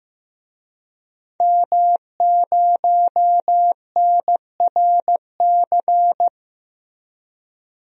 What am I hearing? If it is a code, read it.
Morse code, heard as a keyed tone, about 15 wpm, 700 Hz. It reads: M0NRC
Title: DXCC-100 Call Signs - 5 Characters - Repeated 3x - 15wpm